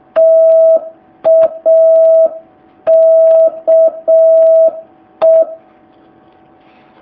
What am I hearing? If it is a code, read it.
Morse code, heard as a keyed tone, about 6 wpm, 650 Hz. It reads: TAKE
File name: jh8cbh / 2019cw